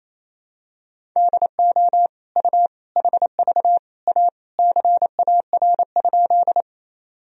Morse code, heard as a keyed tone, about 28 wpm, 700 Hz. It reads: DO U HV A CAR?
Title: Sentences from Top 300 Words and Abbreviations 28wpm